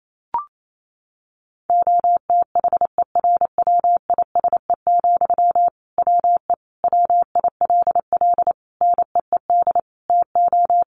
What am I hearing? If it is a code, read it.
Morse code, heard as a keyed tone, about 28 wpm, 700 Hz. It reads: OTHERWISE, WE WILL NEED TO
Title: Sentences from Top 1500 Words 28wpm